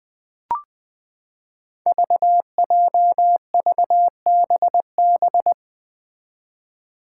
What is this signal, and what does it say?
Morse code, heard as a keyed tone, about 21 wpm, 700 Hz. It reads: VJVBB
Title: Five Challenging Letter-Number Combinations 20wpm